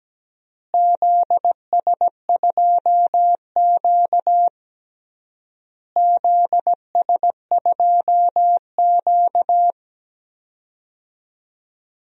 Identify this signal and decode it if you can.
Morse code, heard as a keyed tone, about 17 wpm, 700 Hz. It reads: ZS2Q ZS2Q
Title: DXCC-100 Call Signs - 4 Characters - Repeated 3x - 17wpm